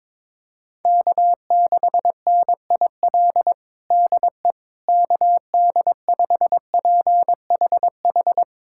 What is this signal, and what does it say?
Morse code, heard as a keyed tone, about 22 wpm, 700 Hz. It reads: K6NIL DE KD5PHH
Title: USA Callsign DE Callsign 22wpm